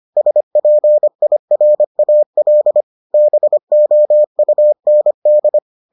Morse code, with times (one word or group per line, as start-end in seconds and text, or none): 0.2-2.8 SPIRAL
3.1-5.6 BOUND